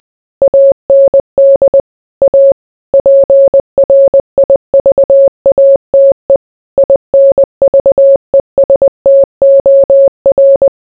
Morse code, heard as a keyed tone, about 20 wpm, 550 Hz. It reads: AND A PRIVATE INVESTOR